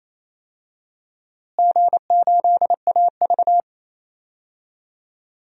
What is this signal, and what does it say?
Morse code, heard as a keyed tone, about 28 wpm, 700 Hz. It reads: G8AV
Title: DXCC-100 Call Signs - 4 Characters - Repeated 3x - 28wpm